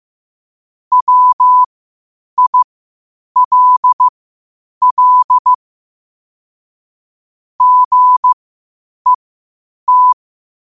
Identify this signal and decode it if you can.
Morse code, heard as a keyed tone, about 15 wpm, spaced as for 8 wpm, 1000 Hz. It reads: WILL GET
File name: 8wpm